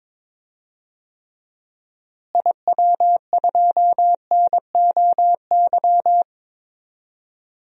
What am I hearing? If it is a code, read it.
Morse code, heard as a keyed tone, about 22 wpm, 700 Hz. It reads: IW2NOY
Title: DXCC-100 Call Signs - 6 Characters - Repeated 2x - 22wpm